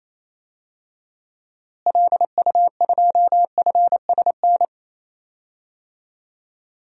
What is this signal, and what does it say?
Morse code, heard as a keyed tone, about 28 wpm, 700 Hz. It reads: LU2FSN
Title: DXCC-100 Call Signs - 6 Characters - Repeated 3x - 28wpm